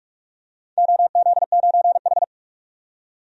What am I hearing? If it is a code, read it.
Morse code, heard as a keyed tone, about 42 wpm, 700 Hz. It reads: OZ9H
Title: DXCC-100 Call Signs - 4 Characters - 45wpm